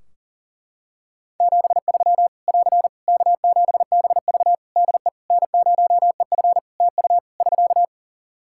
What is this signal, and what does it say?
Morse code, heard as a keyed tone, about 40 wpm, 700 Hz. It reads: 73 <AR> K7BV DE N0EF TU <SK>
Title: Practice QSO - Exchange 4 - Ending - 40wpm